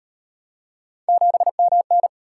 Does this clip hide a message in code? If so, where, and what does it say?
Yes: Morse code from 1.1 to 2.1 s, heard as a keyed tone, about 37 wpm, 700 Hz. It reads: 7MN